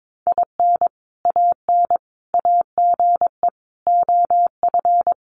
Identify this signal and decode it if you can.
Morse code, heard as a keyed tone, about 22 wpm, 700 Hz. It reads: IN AN AGE OF
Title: Sentences from Top 1200 Words 22wpm